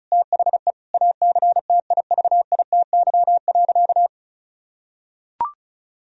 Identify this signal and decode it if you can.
Morse code, heard as a keyed tone, about 36 wpm, 700 Hz. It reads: THE ACTIVITY.
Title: Sentences from Top 1000 Words - Review Part 4 - 35wpm